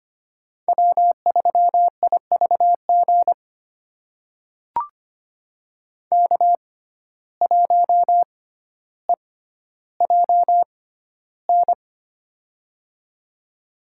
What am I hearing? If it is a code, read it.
Morse code, heard as a keyed tone, about 25 wpm, spaced as for 13 wpm, 700 Hz. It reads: W3IVG K1EJN
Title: USA Call Signs - 5 Characters - 6x Character Spacing - 25wpm